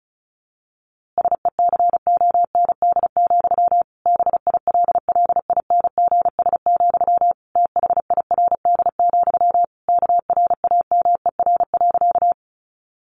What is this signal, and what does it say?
Morse code, heard as a keyed tone, about 34 wpm, 700 Hz. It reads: SECOND, BILLINGS, THIRD, KRAMER.
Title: A Sound Of Thunder 35wpm